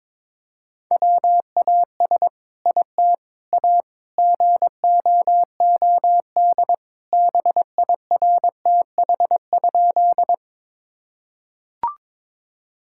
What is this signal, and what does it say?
Morse code, heard as a keyed tone, about 22 wpm, 700 Hz. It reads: WAS IT A GOOD BIRTH?
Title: Sentences from Top 1000 Words - Review Part 5 - 22wpm